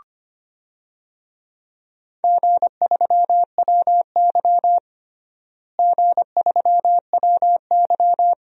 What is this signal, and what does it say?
Morse code, heard as a keyed tone, about 25 wpm, 700 Hz. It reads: G3WY G3WY
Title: DXCC-100 Call Signs - 4 Characters - Repeated 3x - 25wpm